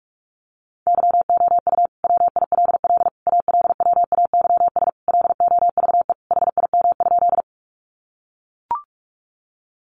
Morse code, heard as a keyed tone, about 44 wpm, 700 Hz. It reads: YOU WILL ALWAYS LOVE HIM?
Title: Sentences from Top 400 Words 45wpm